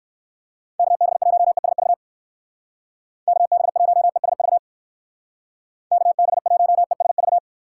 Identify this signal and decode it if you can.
Morse code, heard as a keyed tone, about 56 wpm, 700 Hz. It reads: KB1ESV KB1ESV KB1ESV
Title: Speed Racing - USA 2x3 Call Signs 45wpm